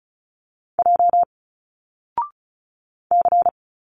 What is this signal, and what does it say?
Morse code, heard as a keyed tone, about 34 wpm, 700 Hz. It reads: J C
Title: Single Letters 35wpm